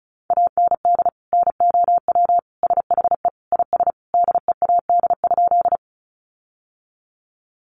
Morse code, heard as a keyed tone, about 34 wpm, 700 Hz. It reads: AND NOW SHE IS DEAD?
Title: Sentences from Top 1500 Words 35wpm